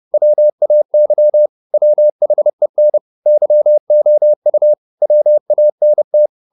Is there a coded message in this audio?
Yes: Morse code from 0.1 to 6.3 s, heard as a keyed tone, about 30 wpm, 600 Hz. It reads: WAY WHEN YOU WANT